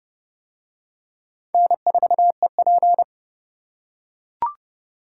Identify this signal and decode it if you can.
Morse code, heard as a keyed tone, about 29 wpm, 700 Hz. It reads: N4EP